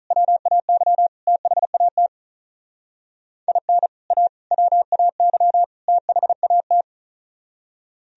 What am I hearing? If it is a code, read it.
Morse code, heard as a keyed tone, about 44 wpm, 700 Hz. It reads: WAY THAT IN A WAY THAT